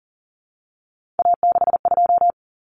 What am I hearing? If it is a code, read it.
Morse code, heard as a keyed tone, about 40 wpm, 700 Hz. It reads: A62
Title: Common Cut Numbers - 3-Digit 40wpm